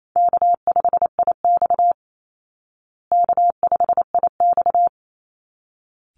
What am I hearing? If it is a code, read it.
Morse code, heard as a keyed tone, about 28 wpm, 700 Hz. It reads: K5IX K5IX
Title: Call Signs Repeated 3x - 28wpm